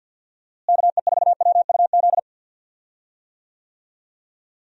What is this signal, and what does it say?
Morse code, heard as a keyed tone, about 48 wpm, 700 Hz. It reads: KE4WUZ
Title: USA Call Signs - 6 Characters - Repeated 3x - 50wpm